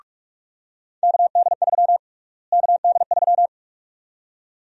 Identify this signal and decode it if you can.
Morse code, heard as a keyed tone, about 44 wpm, 700 Hz. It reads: KD3 KD3
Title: USA Call Sign Prefixes - Repeated 2x - 45wpm